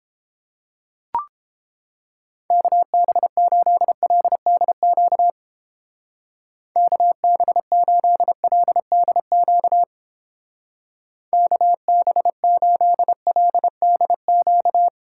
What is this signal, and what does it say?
Morse code, heard as a keyed tone, about 30 wpm, 700 Hz. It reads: KB8LDQ KB8LDQ KB8LDQ